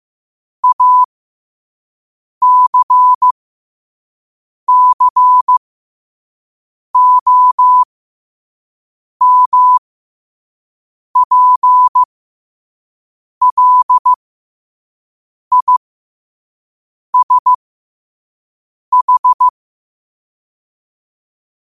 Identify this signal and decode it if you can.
Morse code, heard as a keyed tone, about 15 wpm, spaced as for 5 wpm, 1000 Hz. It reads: ACCOMPLISH
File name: qotdfiles / 5wpm